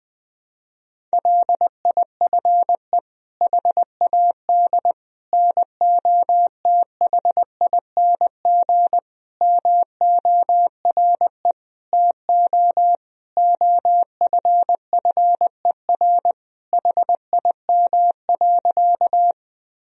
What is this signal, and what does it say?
Morse code, heard as a keyed tone, about 20 wpm, 700 Hz. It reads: LIFE HAD NOTHING MORE TO OFFER HIM.
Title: Sentences from Top 1000 Words - Review Part 4 - 20wpm